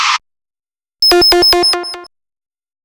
There is no code.